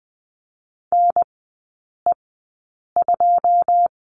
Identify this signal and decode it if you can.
Morse code, heard as a keyed tone, about 20 wpm, spaced as for 8 wpm, 700 Hz. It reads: NE2
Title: 081 Lesson 14e - Number 2 - Three Characters 20wpm